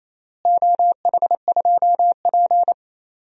Morse code, heard as a keyed tone, about 28 wpm, 700 Hz. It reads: OH2P
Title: DXCC-100 Call Signs - 4 Characters - Repeated 3x - 28wpm